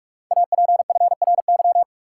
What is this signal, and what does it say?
Morse code, heard as a keyed tone, about 44 wpm, 700 Hz. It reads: APFRY
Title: Code Groups - 5 Letters - 45wpm